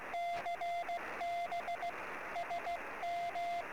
Morse code, heard as a keyed tone, about 16 wpm, 690 Hz, with radio noise. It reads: CB SM
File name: sm7lcb / 08 sm5le-p jo98ax 23cm